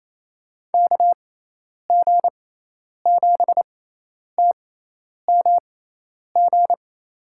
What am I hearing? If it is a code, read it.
Morse code, heard as a keyed tone, about 28 wpm, spaced as for 10 wpm, 700 Hz. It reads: KG7TMG
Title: USA Call Signs - 6 Characters - 6x Character Spacing - 28wpm